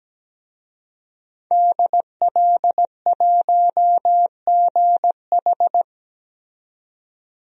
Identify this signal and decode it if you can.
Morse code, heard as a keyed tone, about 17 wpm, 700 Hz. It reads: DL1GH
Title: DXCC-100 Call Signs - 5 Characters - 17wpm